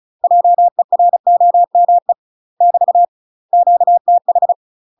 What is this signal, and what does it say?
Morse code, heard as a keyed tone, about 36 wpm, 700 Hz. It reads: JEROME <BT> QTH
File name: UK MASTER QSOs TRAINING / 35WPM / UK MASTER QSOs 0026